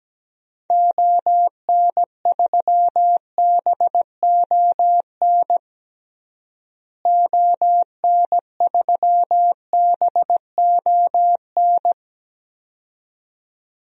Morse code, heard as a keyed tone, about 17 wpm, 700 Hz. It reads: ON3BON ON3BON
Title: DXCC-100 Call Signs - 6 Characters - Repeated 2x - 17wpm